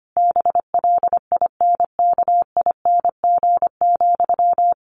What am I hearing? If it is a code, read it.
Morse code, heard as a keyed tone, about 25 wpm, 700 Hz. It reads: BLINKING,
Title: A Sound Of Thunder 25wpm